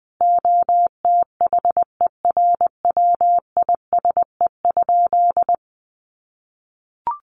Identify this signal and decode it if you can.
Morse code, heard as a keyed tone, about 20 wpm, 700 Hz. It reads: OTHERWISE?
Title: Sentences from Top 1200 Words 20wpm